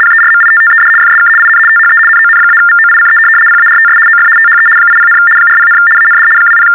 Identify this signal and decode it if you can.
RTTY, about 75 baud. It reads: PERMITTED FSK IN THE NON-VOICE PARTS OF THE 80, 40, AND 20
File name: teletype-RTTY-75